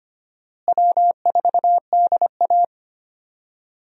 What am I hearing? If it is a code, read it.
Morse code, heard as a keyed tone, about 25 wpm, 700 Hz. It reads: W4DA
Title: USA Call Signs - 4 Characters - Repeated 2x - 25wpm